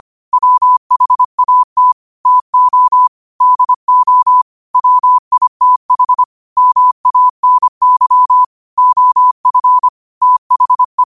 Morse code, heard as a keyed tone, about 25 wpm, 1000 Hz. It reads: WHAT TO DO WITH MANY OF THE